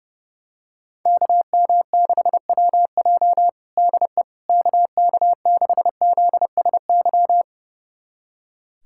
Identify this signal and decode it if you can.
Morse code, heard as a keyed tone, about 30 wpm, 700 Hz. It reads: KM6WJ DE KK6ZSY
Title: USA Callsign DE Callsign 30wpm